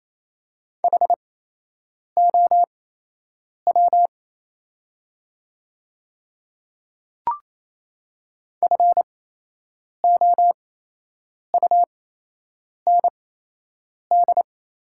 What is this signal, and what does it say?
Morse code, heard as a keyed tone, about 28 wpm, spaced as for 8 wpm, 700 Hz. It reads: HOW FOUND
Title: 8x Character Spacing - Top 300 Words - 28wpm